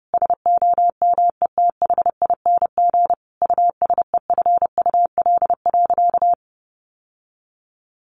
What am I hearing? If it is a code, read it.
Morse code, heard as a keyed tone, about 30 wpm, 700 Hz. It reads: SOMETHING USEFUL.
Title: Sentences from Top 1400 Words 30wpm